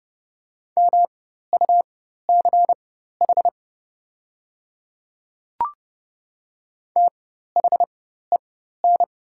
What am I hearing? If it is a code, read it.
Morse code, heard as a keyed tone, about 30 wpm, spaced as for 14 wpm, 700 Hz. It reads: MUCH THEN
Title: 4x Character Spacing - Top 200 Words - 30wpm